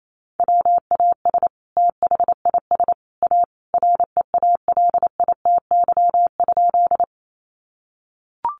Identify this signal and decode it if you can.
Morse code, heard as a keyed tone, about 28 wpm, 700 Hz. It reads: WAS THIS A REALITY?
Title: Sentences from Top 1200 Words 28wpm